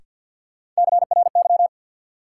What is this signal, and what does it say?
Morse code, heard as a keyed tone, about 50 wpm, 700 Hz. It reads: CRY
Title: Sets of 5 Words Encore - New Word Drill - 50wpm